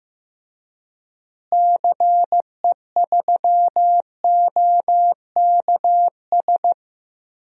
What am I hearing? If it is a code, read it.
Morse code, heard as a keyed tone, about 15 wpm, 700 Hz. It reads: CE3OKS
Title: DXCC-100 Call Signs - 6 Characters - Repeated 2x - 15wpm